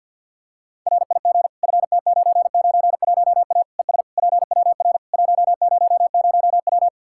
Morse code, heard as a keyed tone, about 51 wpm, 700 Hz. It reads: RIG FT991A ES PWR 100W
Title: Practice QSO - Exchange 2 - Info - 50wpm